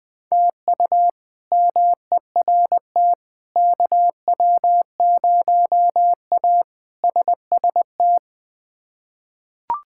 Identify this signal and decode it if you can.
Morse code, heard as a keyed tone, about 21 wpm, 700 Hz. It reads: TU MERT KW0A SST
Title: K1USN SST QSO Practice - 20wpm